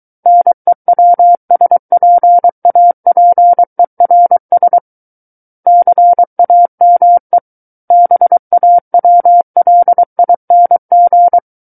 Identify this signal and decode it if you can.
Morse code, heard as a keyed tone, about 23 wpm, 700 Hz. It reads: NEWSPAPERS CAME BAWLING
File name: Book One / 14-Chapter 14 - 23 WPM